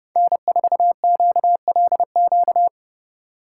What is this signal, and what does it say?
Morse code, heard as a keyed tone, about 35 wpm, 700 Hz. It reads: N4QLQ
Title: Speed Racing - USA 1x3 Call Signs 30wpm